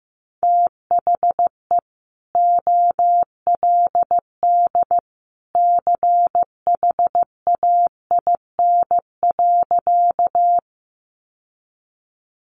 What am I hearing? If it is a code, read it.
Morse code, heard as a keyed tone, about 15 wpm, 700 Hz. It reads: THE OLD CHAIN.
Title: Sentences from Top 1500 Words 15wpm